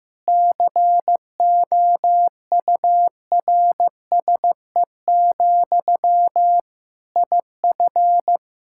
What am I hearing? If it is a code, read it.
Morse code, heard as a keyed tone, about 15 wpm, 700 Hz. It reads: COURSE, IF
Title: Sentences from Top 1600 Words 15wpm